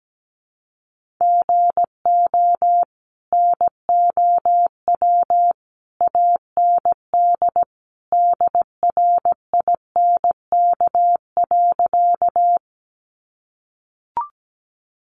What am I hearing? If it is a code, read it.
Morse code, heard as a keyed tone, about 17 wpm, 700 Hz. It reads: GO NOW AND DRINK.
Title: Sentences from Top 1500 Words 17wpm